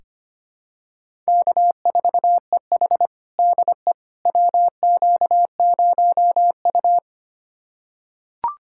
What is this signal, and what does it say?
Morse code, heard as a keyed tone, about 25 wpm, 700 Hz. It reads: K4EH DE WQ0U
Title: USA Callsign DE Callsign - 4 Characters - 25wpm